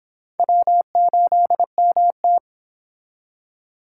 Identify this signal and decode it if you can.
Morse code, heard as a keyed tone, about 26 wpm, 700 Hz. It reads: W8MT